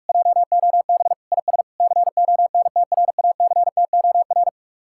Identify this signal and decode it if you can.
Morse code, heard as a keyed tone, about 49 wpm, 700 Hz. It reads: JOB IS CONTRACTOR